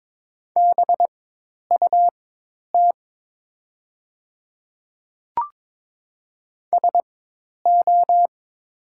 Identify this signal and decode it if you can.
Morse code, heard as a keyed tone, about 22 wpm, spaced as for 10 wpm, 700 Hz. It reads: BUT SO
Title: 4x Character Spacing - Top 100 Words - 22wpm